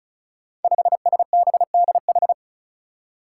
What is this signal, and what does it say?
Morse code, heard as a keyed tone, about 36 wpm, 700 Hz. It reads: 5SBDH
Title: Five Challenging Letter-Number Combinations 35wpm